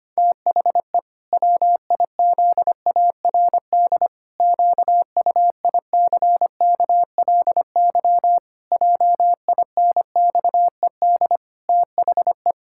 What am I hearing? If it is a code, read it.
Morse code, heard as a keyed tone, about 25 wpm, 700 Hz. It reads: THE WIZARD QUICKLY JINXED THE
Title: Pangram Challenge v2 - 25wpm